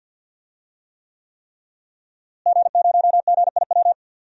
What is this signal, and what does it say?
Morse code, heard as a keyed tone, about 52 wpm, 700 Hz. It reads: G0GIW
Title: DXCC-100 Call Signs - 5 Characters - Repeated 3x - 50wpm